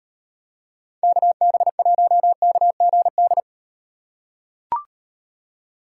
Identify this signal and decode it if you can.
Morse code, heard as a keyed tone, about 38 wpm, 700 Hz. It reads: KB1KGD